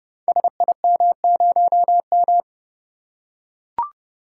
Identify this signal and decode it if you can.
Morse code, heard as a keyed tone, about 30 wpm, 700 Hz. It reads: SIM0M
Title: Code Groups - 5 Letter:Numbers - 30wpm